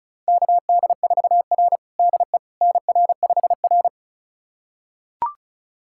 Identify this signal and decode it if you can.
Morse code, heard as a keyed tone, about 34 wpm, 700 Hz. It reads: KD4R DE NR5R
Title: USA Callsign DE Callsign - 4 Characters - 35wpm